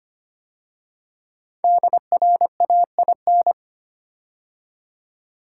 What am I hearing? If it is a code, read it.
Morse code, heard as a keyed tone, about 25 wpm, 700 Hz. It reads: DRAIN
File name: 5 Letter Words - First 3 Letters Given - 25wpm